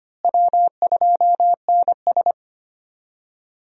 Morse code, heard as a keyed tone, about 25 wpm, 700 Hz. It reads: W2NS
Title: USA Call Signs - 4 Characters - Repeated 2x - 25wpm